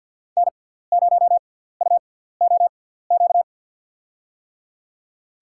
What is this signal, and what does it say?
Morse code, heard as a keyed tone, about 53 wpm, spaced as for 17 wpm, 700 Hz. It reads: N0UOQ
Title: USA Call Signs - 5 Characters - 6x Character Spacing - 50wpm